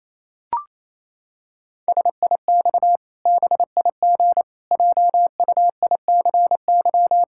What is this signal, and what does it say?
Morse code, heard as a keyed tone, about 28 wpm, 700 Hz. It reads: SIX BIG JUICY
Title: Pangram Challenge v2 - 28wpm